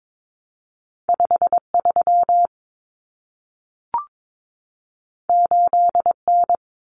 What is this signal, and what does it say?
Morse code, heard as a keyed tone, about 22 wpm, 700 Hz. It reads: 53 8N
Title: Common Cut Numbers - 2-Digit 22wpm